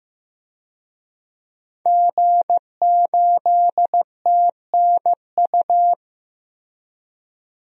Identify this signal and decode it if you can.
Morse code, heard as a keyed tone, about 15 wpm, 700 Hz. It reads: G8TNU
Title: DXCC-100 Call Signs - 5 Characters - Repeated 3x - 15wpm